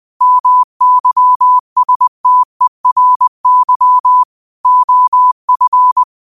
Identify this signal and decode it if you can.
Morse code, heard as a keyed tone, about 21 wpm, 1000 Hz. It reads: MYSTERY OF